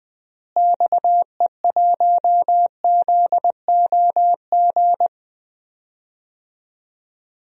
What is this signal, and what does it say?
Morse code, heard as a keyed tone, about 20 wpm, 700 Hz. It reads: XE1ZOG
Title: DXCC-100 Call Signs - 6 Characters - Repeated 3x - 20wpm